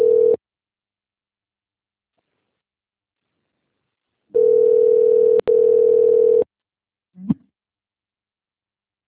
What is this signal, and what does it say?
Ringback tone (North America), 440 + 480 Hz.